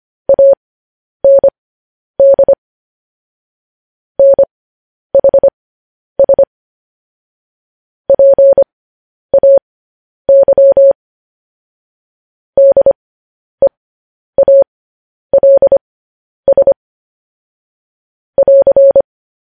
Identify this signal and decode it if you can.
Morse code, heard as a keyed tone, about 25 wpm, spaced as for 10 wpm, 550 Hz. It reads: AND NHS PAY DEALS <AR>
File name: podcast / News headlines-1x-25x10